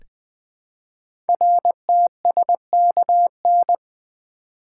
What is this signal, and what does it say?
Morse code, heard as a keyed tone, about 20 wpm, 700 Hz. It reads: RTSKN
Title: Code Groups - 5 Letters - 20wpm